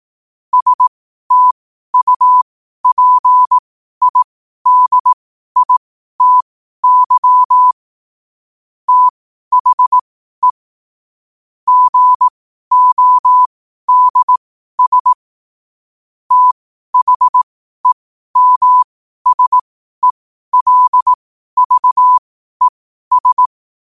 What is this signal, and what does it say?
Morse code, heard as a keyed tone, about 18 wpm, spaced as for 12 wpm, 1000 Hz. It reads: STUPIDITY THE GODS THEMSELVES
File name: qotdfiles / 12wpm